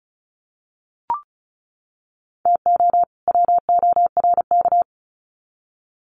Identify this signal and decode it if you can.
Morse code, heard as a keyed tone, about 34 wpm, 700 Hz. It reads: TO WORK
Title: Sets of 2 Words 35wpm